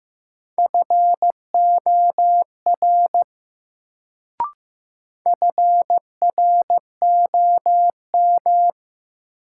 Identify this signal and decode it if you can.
Morse code, heard as a keyed tone, about 15 wpm, 700 Hz. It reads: FOR FROM
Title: Sentences from Top 100 Words - New Word Drill - 15wpm